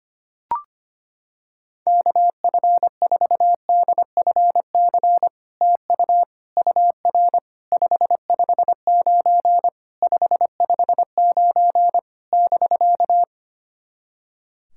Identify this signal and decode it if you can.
Morse code, heard as a keyed tone, about 25 wpm, 700 Hz. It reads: KF4DFC TU UR 559 559 <BK>
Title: SOTA QSO Practice 25wpm